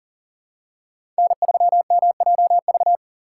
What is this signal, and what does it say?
Morse code, heard as a keyed tone, about 42 wpm, 700 Hz. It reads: N3MJV
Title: USA Call Signs - 6x Character Spacing - 40wpm